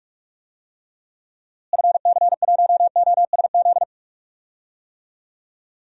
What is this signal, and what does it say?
Morse code, heard as a keyed tone, about 46 wpm, 700 Hz. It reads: FC1OSZ
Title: DXCC-100 Call Signs - 6 Characters - Repeated 3x - 45wpm